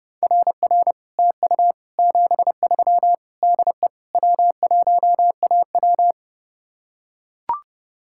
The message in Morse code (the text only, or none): RR TU 73 DE W1AW